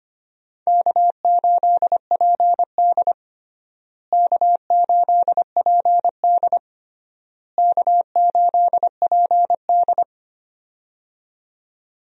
Morse code, heard as a keyed tone, about 25 wpm, 700 Hz. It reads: K8PD K8PD K8PD
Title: USA Call Signs - 4 Characters - Repeated 3x - 25wpm